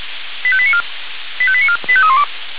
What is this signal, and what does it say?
Five-tone selective call (ZVEI), heard as quick runs of five tones, tones of about 70 ms. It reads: 85894 85894 85312
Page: SDR/ZVEI